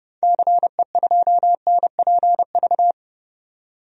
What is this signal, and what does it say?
Morse code, heard as a keyed tone, about 30 wpm, 700 Hz. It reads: CE2NPV
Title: DXCC-100 Call Signs - 6 Characters - 30wpm